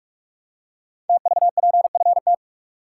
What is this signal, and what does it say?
Morse code, heard as a keyed tone, about 44 wpm, 700 Hz. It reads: TVPFT
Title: Code Groups - 5 Letters - 45wpm